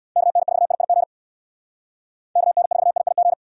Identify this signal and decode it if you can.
Morse code, heard as a keyed tone, about 78 wpm, 700 Hz. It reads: KN4IIG KN4IIG